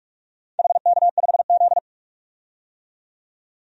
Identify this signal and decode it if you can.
Morse code, heard as a keyed tone, about 46 wpm, 700 Hz. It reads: HK5Z
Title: DXCC-100 Call Signs - 4 Characters - 45wpm